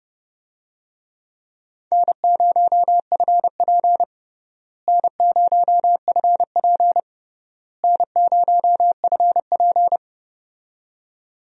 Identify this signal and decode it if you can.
Morse code, heard as a keyed tone, about 30 wpm, 700 Hz. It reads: N0FP N0FP N0FP
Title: USA Call Signs - 4 Characters - Repeated 3x - 30wpm